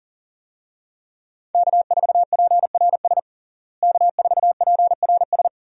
Morse code, heard as a keyed tone, about 40 wpm, 700 Hz. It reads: K4PRS K4PRS
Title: USA Call Signs - 5 Characters - Repeated 3x - 40wpm